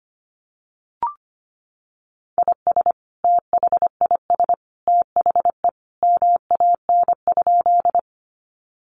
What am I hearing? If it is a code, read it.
Morse code, heard as a keyed tone, about 25 wpm, 700 Hz. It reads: IS THIS THE MAN?
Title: Sentences from Top 200 Words 25wpm